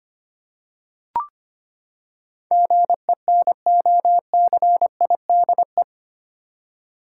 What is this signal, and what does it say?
Morse code, heard as a keyed tone, about 25 wpm, 700 Hz. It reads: GENOCIDE
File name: Eight Letter Words - 25wpm